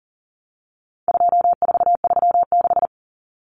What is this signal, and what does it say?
Morse code, heard as a keyed tone, about 40 wpm, 700 Hz. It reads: 2436
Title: Numbers - 4-Digit 40wpm